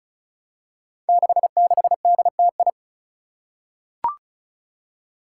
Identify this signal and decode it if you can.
Morse code, heard as a keyed tone, about 36 wpm, 700 Hz. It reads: 66DTI
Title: Five Challenging Letter-Number Combinations 35wpm